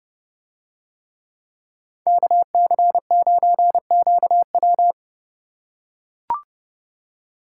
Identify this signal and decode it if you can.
Morse code, heard as a keyed tone, about 27 wpm, 700 Hz. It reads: KC9QW